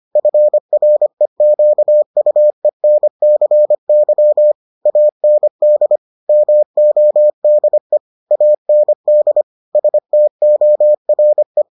Morse code, heard as a keyed tone, about 25 wpm, 600 Hz. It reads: FREQUENCY AND MODE AND STORE